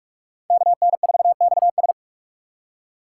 Morse code, heard as a keyed tone, about 45 wpm, 700 Hz. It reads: KN4XS
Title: USA Call Signs - 5 Characters - 6x Character Spacing - 45wpm